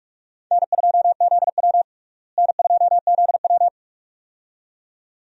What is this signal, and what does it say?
Morse code, heard as a keyed tone, about 46 wpm, 700 Hz. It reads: N2ZW N2ZW
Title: USA Call Signs - 4 Characters - Repeated 2x - 45wpm